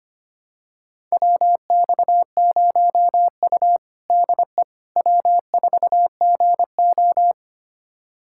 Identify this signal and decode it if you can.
Morse code, heard as a keyed tone, about 25 wpm, 700 Hz. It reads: WX0U DE W4GO
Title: USA Callsign DE Callsign - 4 Characters - 25wpm